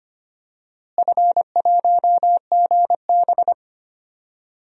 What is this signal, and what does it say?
Morse code, heard as a keyed tone, about 25 wpm, 700 Hz. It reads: F1GB
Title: DXCC-100 Call Signs - 4 Characters - Repeated 2x - 25wpm